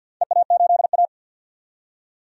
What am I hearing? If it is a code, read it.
Morse code, heard as a keyed tone, about 46 wpm, 700 Hz. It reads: EA8A
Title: DXCC-100 Call Signs - 4 Characters - 50wpm